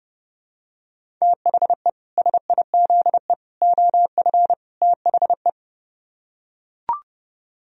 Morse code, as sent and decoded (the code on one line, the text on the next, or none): - .... / ... .. --.. / --- ..-. / - .... .
THE SIZE OF THE